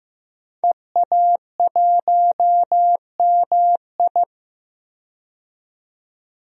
Morse code, heard as a keyed tone, about 15 wpm, 700 Hz. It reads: EA1MI
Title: DXCC-100 Call Signs - 5 Characters - 15wpm